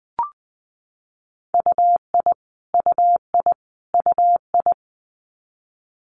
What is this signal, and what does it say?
Morse code, heard as a keyed tone, about 19 wpm, 700 Hz. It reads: UI UI UI
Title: Multi-letter Phonemes Repeated 3x 20wpm